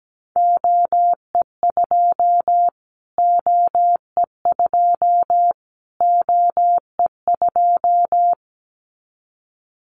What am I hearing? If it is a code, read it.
Morse code, heard as a keyed tone, about 17 wpm, 700 Hz. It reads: OE2 OE2 OE2
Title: Callsign Prefixes for Top 25 DXCC - Repeated 3x - 17wpm